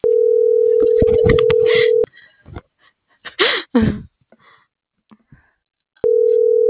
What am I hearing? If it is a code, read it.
Ringback tone (North America), 440 + 480 Hz.